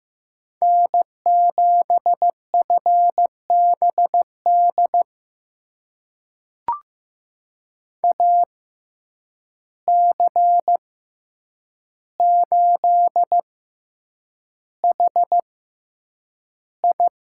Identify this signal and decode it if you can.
Morse code, heard as a keyed tone, about 15 wpm, spaced as for 8 wpm, 700 Hz. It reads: N7FBD AC8HI
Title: USA Call Signs - 5 Characters - 6x Character Spacing - 15wpm